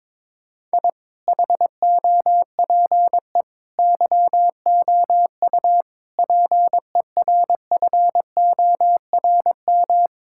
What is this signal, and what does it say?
Morse code, heard as a keyed tone, about 22 wpm, 700 Hz. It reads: I HOPE YOU PERFORM
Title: Sentences from Top 1000 Words - Review Part 6 - 22wpm